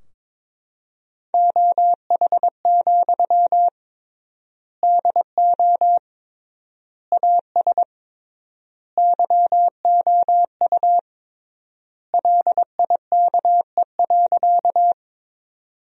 Morse code, heard as a keyed tone, about 22 wpm, 700 Hz. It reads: OH, DO AS YOU LIKE.
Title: 3x Word-Spacing - Sentences from Top 500 Words - 22wpm